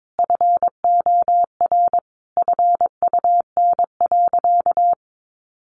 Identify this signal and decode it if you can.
Morse code, heard as a keyed tone, about 22 wpm, 700 Hz. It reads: FOR FUN.
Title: Additional Easy Phrases 22wpm